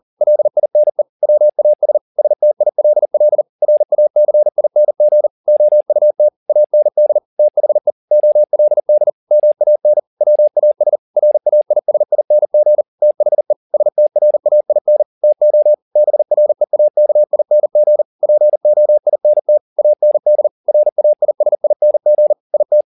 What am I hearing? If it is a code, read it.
Morse code, heard as a keyed tone, about 40 wpm, 600 Hz. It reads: LINE WAS STILL RACING OUT AND THE OLD MAN WAS RAISING THE STRAIN TO BREAKING POINT AND RAISING IT
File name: the old man and the sea 2 40wpm